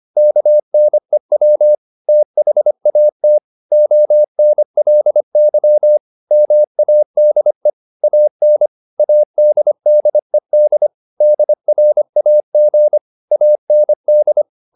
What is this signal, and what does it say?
Morse code, heard as a keyed tone, about 25 wpm, 600 Hz. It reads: KNEW THAT ONLY MADE AN ADDED DRAG AND
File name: the old man and the sea part 1 25wpm